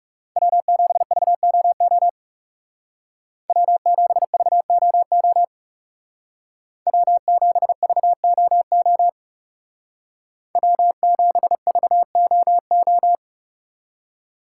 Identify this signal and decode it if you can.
Morse code, heard as a keyed tone, about 38 wpm, 700 Hz. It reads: W7VOO W7VOO W7VOO W7VOO